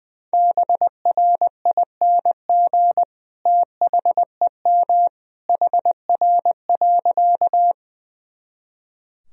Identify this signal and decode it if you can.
Morse code, heard as a keyed tone, about 20 wpm, 700 Hz. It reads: BRING THEM HR.
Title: Sentences from Top 500 Words and Abbreviations 20wpm